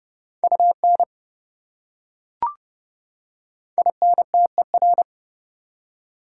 Morse code, heard as a keyed tone, about 27 wpm, 700 Hz. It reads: UN INTER